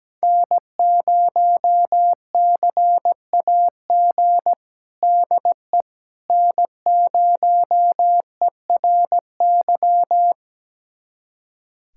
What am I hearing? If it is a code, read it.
Morse code, heard as a keyed tone, about 17 wpm, 700 Hz. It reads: N0CAG DE N0ERY
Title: USA Callsign DE Callsign - 5 Characters - 17wpm